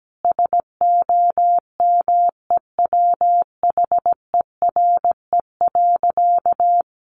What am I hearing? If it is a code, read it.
Morse code, heard as a keyed tone, about 17 wpm, 700 Hz. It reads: SOMEWHERE.
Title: Sentences from Top 1400 Words 17wpm